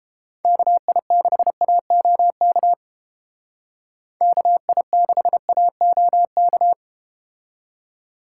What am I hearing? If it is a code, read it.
Morse code, heard as a keyed tone, about 32 wpm, 700 Hz. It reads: KI6AOK KI6AOK